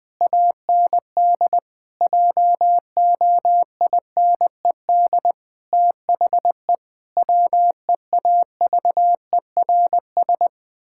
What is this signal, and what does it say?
Morse code, heard as a keyed tone, about 20 wpm, 700 Hz. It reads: AND JOINED THE WEAVERS
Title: Pangram Challenge v2 - 20wpm